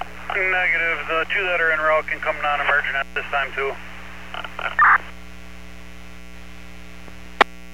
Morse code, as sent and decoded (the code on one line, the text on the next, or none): none